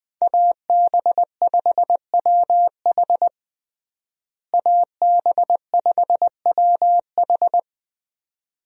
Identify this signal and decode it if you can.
Morse code, heard as a keyed tone, about 21 wpm, 700 Hz. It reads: AB5WH AB5WH
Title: USA Call Signs - 5 Characters - Repeated 2x - 20wpm